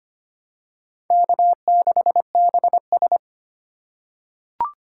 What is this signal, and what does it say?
Morse code, heard as a keyed tone, about 25 wpm, 700 Hz. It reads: K6BS
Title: USA Call Signs - 4 Characters - 3x Character Spacing - 25wpm